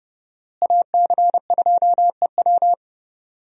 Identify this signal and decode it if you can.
Morse code, heard as a keyed tone, about 30 wpm, 700 Hz. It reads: AC2EW
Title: USA Call Signs - 5 Characters - Repeated 3x - 30wpm